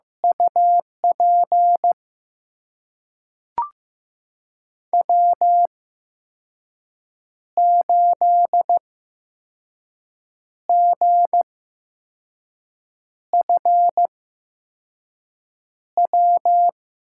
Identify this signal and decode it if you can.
Morse code, heard as a keyed tone, about 15 wpm, spaced as for 5 wpm, 700 Hz. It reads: UP W8GFW
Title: USA Call Signs - 5 Characters - 8x Character Spacing - 15wpm